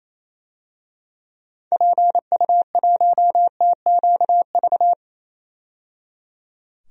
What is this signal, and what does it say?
Morse code, heard as a keyed tone, about 28 wpm, 700 Hz. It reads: PU1TQV
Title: DXCC-100 Call Signs - 6 Characters - 28wpm